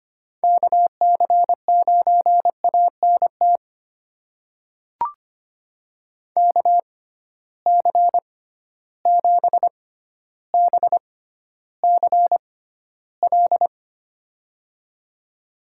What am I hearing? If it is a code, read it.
Morse code, heard as a keyed tone, about 25 wpm, spaced as for 13 wpm, 700 Hz. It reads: KC9ANT KC7BCL
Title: USA Call Signs - 6 Characters - 6x Character Spacing - 25wpm